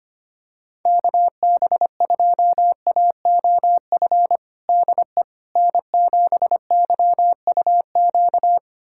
Morse code, heard as a keyed tone, about 25 wpm, 700 Hz. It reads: KB2AOF DE N7YUQ
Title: USA Callsign DE Callsign 25wpm